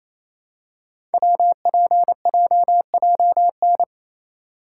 Morse code, heard as a keyed tone, about 28 wpm, 700 Hz. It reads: WPJJN
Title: Code Groups - 5 Letters - 28wpm